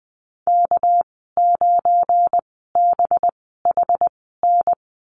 Morse code, heard as a keyed tone, about 20 wpm, spaced as for 14 wpm, 700 Hz. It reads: K9BHN
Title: 213 Lesson 34g - Number 8 - Call Signs 20wpm